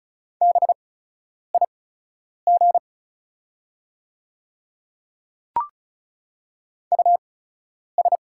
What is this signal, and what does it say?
Morse code, heard as a keyed tone, about 34 wpm, spaced as for 10 wpm, 700 Hz. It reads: BIG US
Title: 8x Character Spacing - Top 200 Words - 35wpm